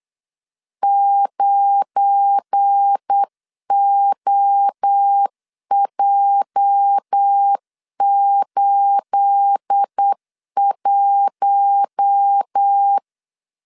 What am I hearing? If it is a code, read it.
Morse code, heard as a keyed tone, about 8 wpm, 800 Hz. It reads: 9OJ81
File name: CW / DET / 1 9 1500C S A 8wpm 5637